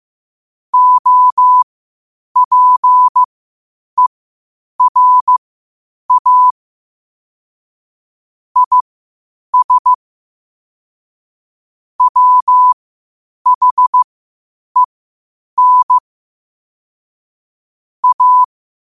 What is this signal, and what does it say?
Morse code, heard as a keyed tone, about 15 wpm, spaced as for 8 wpm, 1000 Hz. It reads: OPERA IS WHEN A